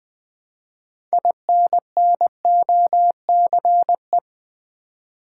Morse code, heard as a keyed tone, about 20 wpm, 700 Hz. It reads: INNOCE
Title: Build A Word - 2 Letters At A Time - 8-Letter Words - 20wpm